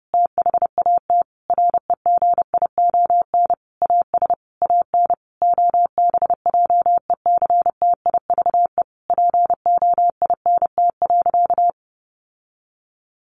Morse code, heard as a keyed tone, about 28 wpm, 700 Hz. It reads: THAT REGION AS AN OBJECTIVE POINT.
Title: Sentences from Top 1500 Words 30wpm